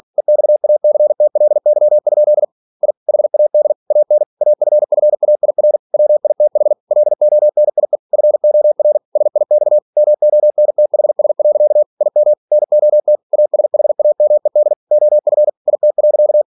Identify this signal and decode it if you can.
Morse code, heard as a keyed tone, about 46 wpm, 600 Hz. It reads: EXACTLY? I HAD AN AFFAIR WITH LONIE FOR SIX MONTHS. IM NOT ASHAMED OF IT.
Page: Never 47wpm